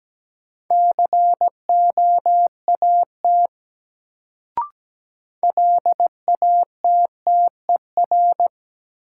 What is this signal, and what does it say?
Morse code, heard as a keyed tone, about 17 wpm, 700 Hz. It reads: COAT LATTER
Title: Sentences from Top 1600 Words - New Word Drill Alt - 17wpm